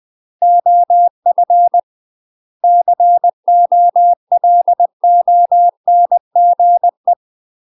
Morse code, heard as a keyed tone, about 20 wpm, 700 Hz. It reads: OF COLONGE
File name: UK MASTER QSOs TRAINING/20WPM